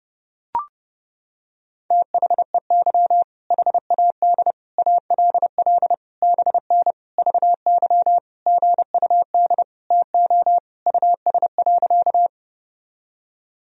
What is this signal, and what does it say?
Morse code, heard as a keyed tone, about 32 wpm, 700 Hz. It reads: THEY HAD ALL BN VY GUD TO US.
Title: Sentences from Top 200 Words and Abbreviations 30wpm